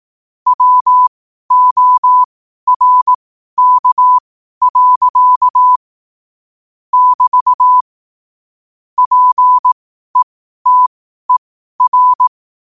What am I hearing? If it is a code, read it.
Morse code, heard as a keyed tone, about 18 wpm, spaced as for 12 wpm, 1000 Hz. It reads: WORK. <BT> PETER
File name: qotdfiles / 12wpm